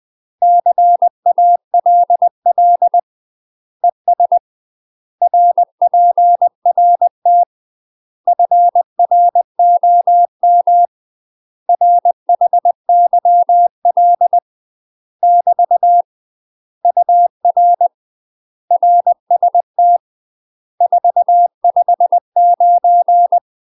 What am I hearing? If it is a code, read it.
Morse code, heard as a keyed tone, about 20 wpm, 700 Hz. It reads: CALL ES RPRT FROM RHYL <BT> UR RST 459
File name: UK MASTER QSOs TRAINING / 20WPM / UK MASTER QSOs 0068